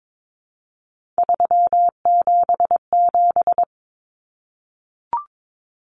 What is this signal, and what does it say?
Morse code, heard as a keyed tone, about 22 wpm, 700 Hz. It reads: 377
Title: Common Cut Numbers - 3-Digit 22wpm